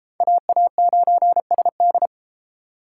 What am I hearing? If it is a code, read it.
Morse code, heard as a keyed tone, about 32 wpm, 700 Hz. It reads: AA9SD